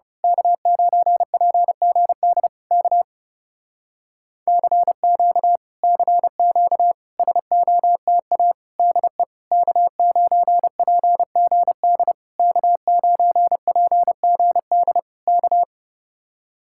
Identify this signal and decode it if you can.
Morse code, heard as a keyed tone, about 30 wpm, 700 Hz. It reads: K9PGD K CQ CQ SOTA DE K9PGD K9PGD K
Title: Speed-Racing Common QSO Phrases - 30wpm